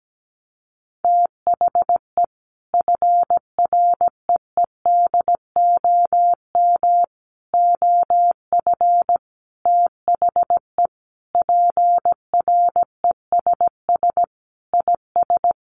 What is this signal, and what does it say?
Morse code, heard as a keyed tone, about 17 wpm, 700 Hz. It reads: THE FREEDOM OF THE PRESS IS
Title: Sentences from Top 800 Words 17wpm